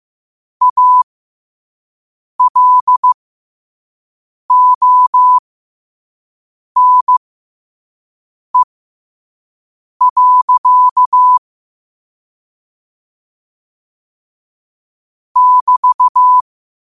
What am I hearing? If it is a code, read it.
Morse code, heard as a keyed tone, about 15 wpm, spaced as for 5 wpm, 1000 Hz. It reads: ALONE. <BT>
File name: qotdfiles / 5wpm